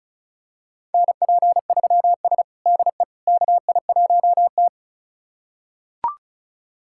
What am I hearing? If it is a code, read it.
Morse code, heard as a keyed tone, about 36 wpm, 700 Hz. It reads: NP3S DE KI1T
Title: USA Callsign DE Callsign - 4 Characters - 35wpm